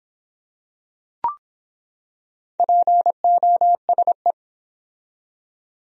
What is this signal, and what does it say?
Morse code, heard as a keyed tone, about 25 wpm, 700 Hz. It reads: POSE